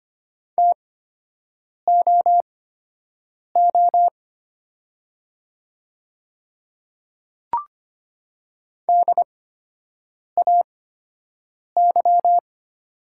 Morse code, heard as a keyed tone, about 25 wpm, spaced as for 7 wpm, 690 Hz. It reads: TOO DAY
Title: 8x Character Spacing - Top 200 Words - 25wpm